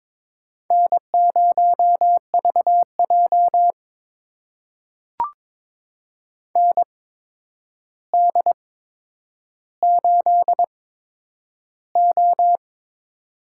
Morse code, heard as a keyed tone, about 22 wpm, spaced as for 9 wpm, 700 Hz. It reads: N0VJ ND8O
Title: USA Call Signs - 4 Characters - 8x Character Spacing - 22wpm